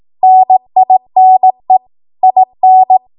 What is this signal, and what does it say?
Morse code, heard as a keyed tone, about 18 wpm, 750 Hz. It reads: NINE IN
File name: Archive / 18 WPM / 18WPM